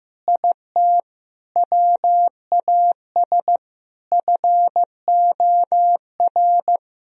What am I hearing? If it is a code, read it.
Morse code, heard as a keyed tone, about 15 wpm, 700 Hz. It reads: IT WAS FOR
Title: Sentences from Top 1000 Words - Review Part 3 - 15wpm